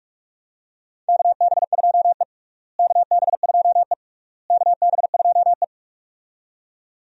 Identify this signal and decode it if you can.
Morse code, heard as a keyed tone, about 46 wpm, 700 Hz. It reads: KB2E KB2E KB2E
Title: USA Call Signs - 4 Characters - Repeated 3x - 45wpm